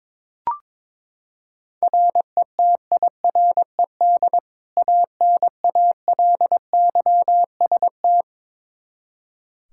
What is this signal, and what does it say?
Morse code, heard as a keyed tone, about 22 wpm, 700 Hz. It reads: RETIRED ANALYST
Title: Practice QSO - Exchange 3 - More Info - 22wpm